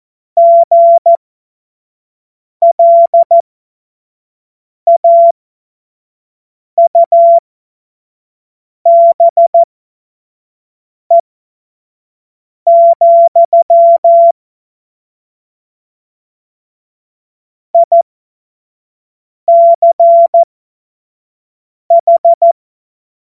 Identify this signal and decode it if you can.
Morse code, heard as a keyed tone, about 14 wpm, spaced as for 5 wpm, 680 Hz. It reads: GLAUBE, ICH
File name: Morsekurs / Witze / Witz 23. August 2022 25 F70